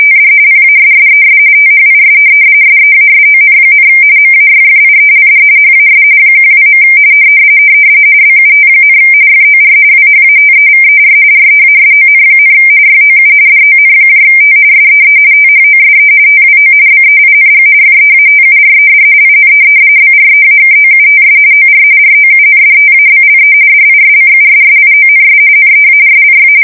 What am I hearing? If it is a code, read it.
RTTY, about 45.45 baud. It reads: TD PASSES THIS SEASON, MAYFIELD HAS 12. BUT, FOR WHATEVER REASON, YOU DONT SEE THOSE TWO BEING FEATURED IN BREAKOUTS OR ROUNDTABLE DISCUSSIONS NEARLY
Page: 170 Shift 2125/2295